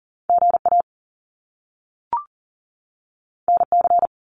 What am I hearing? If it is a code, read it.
Morse code, heard as a keyed tone, about 40 wpm, 700 Hz. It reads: GA NC
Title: US States 40wpm